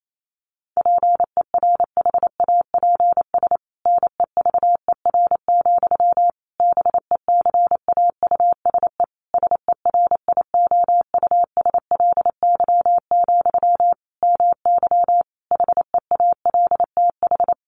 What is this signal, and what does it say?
Morse code, heard as a keyed tone, about 28 wpm, 700 Hz. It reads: PERHAPS NEVER, BECAUSE SERIOUSLY, MY HEALTH
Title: Sentences from Top 1200 Words 28wpm